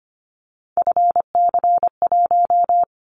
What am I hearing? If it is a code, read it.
Morse code, heard as a keyed tone, about 25 wpm, 700 Hz. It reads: FC1
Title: Callsign Prefixes for Top 50 DXCC - 25wpm